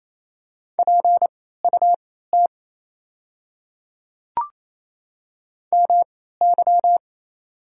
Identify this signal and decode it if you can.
Morse code, heard as a keyed tone, about 28 wpm, spaced as for 16 wpm, 700 Hz. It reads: PUT MY
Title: 3x Character Spacing - Top 200 Words - 28wpm